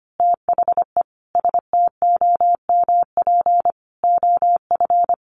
Morse code, heard as a keyed tone, about 25 wpm, 700 Hz. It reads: THE STOMP OF
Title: A Sound Of Thunder 25wpm